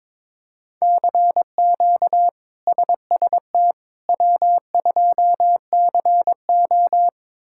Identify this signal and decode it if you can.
Morse code, heard as a keyed tone, about 22 wpm, 700 Hz. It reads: CQ SST W2CO
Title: K1USN SST QSO Practice - 22wpm